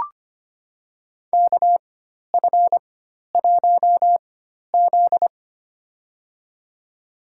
Morse code, heard as a keyed tone, about 25 wpm, spaced as for 12 wpm, 700 Hz. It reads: KF1Z
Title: USA Call Signs - 4x Character Spacing - 25wpm